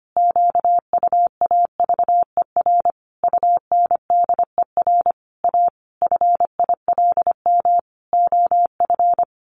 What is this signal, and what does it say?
Morse code, heard as a keyed tone, about 25 wpm, 700 Hz. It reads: QUAVER UNDER A FILM OF
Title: A Sound Of Thunder 25wpm